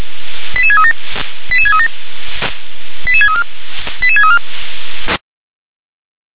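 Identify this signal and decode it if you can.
Five-tone selective call (ZVEI), heard as quick runs of five tones, tones of about 70 ms. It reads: 88637 88637 88634 88634
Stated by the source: ZVEI five-tone call received on an SDR.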